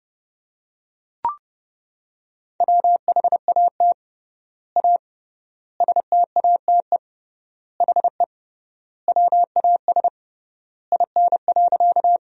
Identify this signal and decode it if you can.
Morse code, heard as a keyed tone, about 30 wpm, 700 Hz. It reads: WHAT A STATE HE WAS IN.
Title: 3x Word-Spacing - Sentences from Top 400 Words - 30wpm